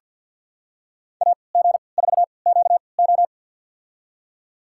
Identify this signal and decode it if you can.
Morse code, heard as a keyed tone, about 51 wpm, spaced as for 28 wpm, 700 Hz. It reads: AG4QO
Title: USA Call Signs - 5 Characters - 3x Character Spacing - 50wpm